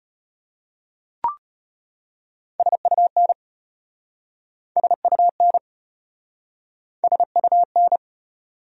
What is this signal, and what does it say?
Morse code, heard as a keyed tone, about 33 wpm, 700 Hz. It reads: SUN SUN SUN